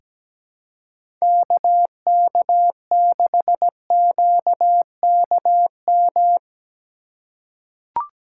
Morse code, heard as a keyed tone, about 17 wpm, 700 Hz. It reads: KK6QKM
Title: USA Call Signs - 6 Characters - 4x Character Spacing - 17wpm